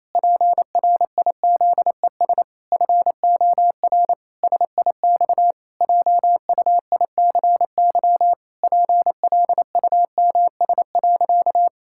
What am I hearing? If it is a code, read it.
Morse code, heard as a keyed tone, about 28 wpm, 700 Hz. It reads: PRIZES FOR SIX JUICY PLUMS.
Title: Pangram Challenge v2 - 28wpm